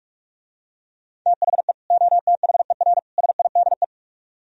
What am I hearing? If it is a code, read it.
Morse code, heard as a keyed tone, about 44 wpm, 700 Hz. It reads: THE OTHER SIDE